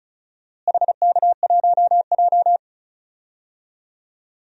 Morse code, heard as a keyed tone, about 34 wpm, 700 Hz. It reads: HK1J
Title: DXCC-100 Call Signs - 4 Characters - Repeated 2x - 35wpm